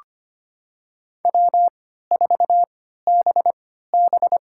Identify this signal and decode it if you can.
Morse code, heard as a keyed tone, about 25 wpm, spaced as for 14 wpm, 700 Hz. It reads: W4BB
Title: USA Call Signs - 4 Characters - 3x Character Spacing - 25wpm